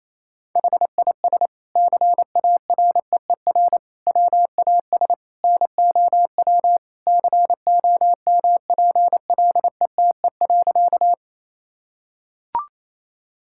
Morse code, heard as a keyed tone, about 28 wpm, 700 Hz. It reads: HIS CAREER WAS NOW COMPLETE.
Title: Sentences from Top 1000 Words - Review Part 6 - 28wpm